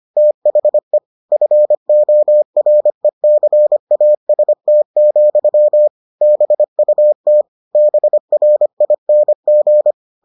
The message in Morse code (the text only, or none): THE FORECAST, BUT BRING